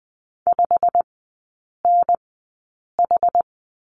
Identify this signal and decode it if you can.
Morse code, heard as a keyed tone, about 20 wpm, spaced as for 8 wpm, 700 Hz. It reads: 5NH
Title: 093 Lesson 16e - Letter c - Three Characters 20wpm